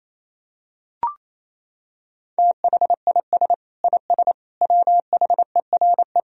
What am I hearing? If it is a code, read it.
Morse code, heard as a keyed tone, about 28 wpm, 700 Hz. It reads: THIS IS WHERE
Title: Sentences from Top 1000 Words - Review Part 3 - 28wpm